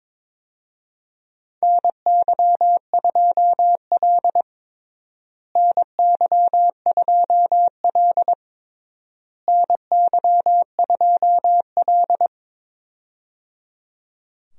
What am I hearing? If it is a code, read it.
Morse code, heard as a keyed tone, about 22 wpm, 700 Hz. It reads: NY2L NY2L NY2L
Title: USA Call Signs - 4 Characters - Repeated 3x - 22wpm